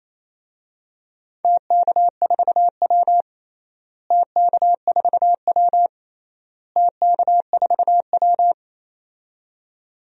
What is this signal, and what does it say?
Morse code, heard as a keyed tone, about 28 wpm, 700 Hz. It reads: TK4W TK4W TK4W
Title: DXCC-100 Call Signs - 4 Characters - Repeated 3x - 28wpm